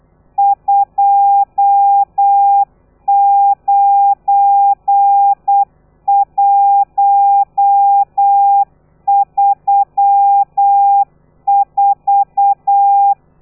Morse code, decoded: 29134